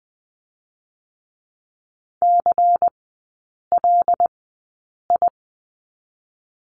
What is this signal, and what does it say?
Morse code, heard as a keyed tone, about 20 wpm, 700 Hz. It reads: C L I
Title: 107 Lesson 18e - Letter m - Three Characters 20wpm